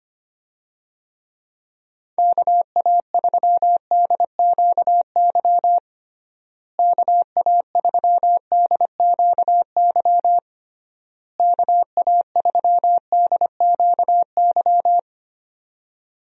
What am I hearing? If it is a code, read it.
Morse code, heard as a keyed tone, about 25 wpm, 700 Hz. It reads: KA3DQY KA3DQY KA3DQY
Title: USA Call Signs - 6 Characters - Repeated 3x - 25wpm